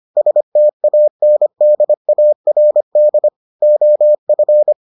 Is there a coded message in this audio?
Yes: Morse code from 0.2 to 4.7 s, heard as a keyed tone, about 25 wpm, 600 Hz. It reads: STANDARD OF